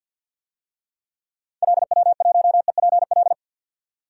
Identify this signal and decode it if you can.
Morse code, heard as a keyed tone, about 48 wpm, 700 Hz. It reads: LW1EPL
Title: DXCC-100 Call Signs - 6 Characters - 50wpm